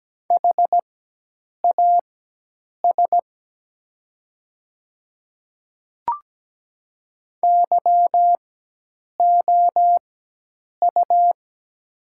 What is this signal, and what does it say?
Morse code, heard as a keyed tone, about 17 wpm, spaced as for 8 wpm, 700 Hz. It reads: HAS YOU
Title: 4x Character Spacing - Top 100 Words - 17wpm